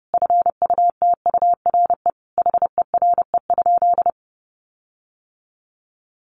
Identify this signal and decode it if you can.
Morse code, heard as a keyed tone, about 30 wpm, 700 Hz. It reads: FUTURE HERE?
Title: Sentences from Top 1500 Words 30wpm